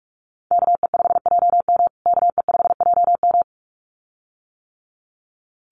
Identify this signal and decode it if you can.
Morse code, heard as a keyed tone, about 44 wpm, 700 Hz. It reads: KE5JM KE5JM
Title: USA Call Signs Repeated 2x 45wpm